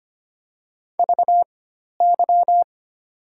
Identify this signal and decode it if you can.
Morse code, heard as a keyed tone, about 25 wpm, spaced as for 12 wpm, 700 Hz. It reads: VY
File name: Top 90 QSO Elements - 4x Character Spacing - 25wpm